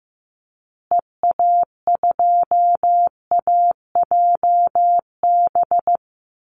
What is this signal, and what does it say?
Morse code, heard as a keyed tone, about 14 wpm, 700 Hz. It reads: EA2AJB
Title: Call Signs Repeated 2x - 15wpm